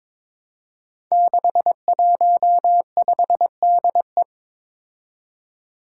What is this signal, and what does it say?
Morse code, heard as a keyed tone, about 22 wpm, 700 Hz. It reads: 615DE
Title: Five Challenging Letter-Number Combinations 22wpm